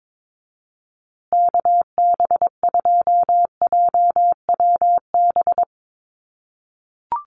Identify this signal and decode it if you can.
Morse code, heard as a keyed tone, about 22 wpm, 700 Hz. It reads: KB2JWB
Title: USA Call Signs Repeated 3x 22wpm